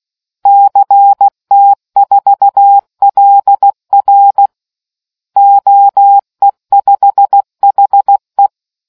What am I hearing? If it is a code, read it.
Morse code, heard as a keyed tone, about 16 wpm, 800 Hz. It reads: CT4LR OE5HE